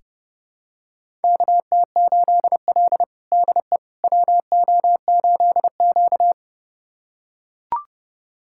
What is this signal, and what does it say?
Morse code, heard as a keyed tone, about 30 wpm, 700 Hz. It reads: KT8L DE WO8Q
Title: USA Callsign DE Callsign - 4 Characters - 30wpm